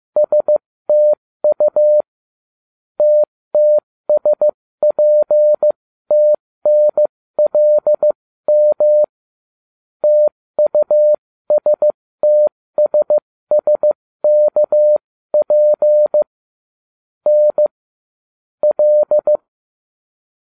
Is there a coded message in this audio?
Yes: Morse code from 0.2 to 19.4 s, heard as a keyed tone, about 15 wpm, spaced as for 13 wpm, 600 Hz. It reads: STU TTSPTNLM TUSTSSKP N L